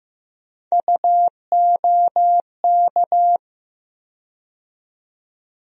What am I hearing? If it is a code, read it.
Morse code, heard as a keyed tone, about 15 wpm, 700 Hz. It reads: UOK
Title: USA Call Sign Postfixes - 3 Letters 15wpm